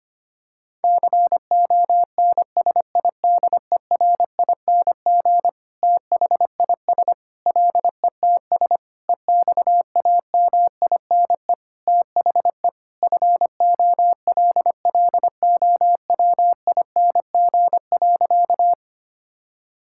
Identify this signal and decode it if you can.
Morse code, heard as a keyed tone, about 25 wpm, 700 Hz. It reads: CONSIDERING THIS LETS EXAMINE THE FOLLOWING.
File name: practice / 25wpm / Sentences with 9-Word Phrases - 25wpm